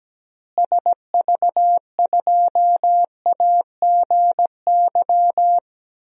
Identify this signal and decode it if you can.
Morse code, heard as a keyed tone, about 17 wpm, 700 Hz. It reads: SV2AGY
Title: DXCC-100 Call Signs - 6 Characters - Repeated 2x - 17wpm